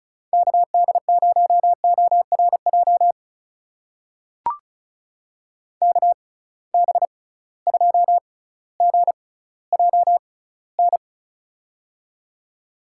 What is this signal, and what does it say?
Morse code, heard as a keyed tone, about 35 wpm, spaced as for 18 wpm, 700 Hz. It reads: KD0ORJ KB2GJN
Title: USA Call Signs - 6 Characters - 6x Character Spacing - 35wpm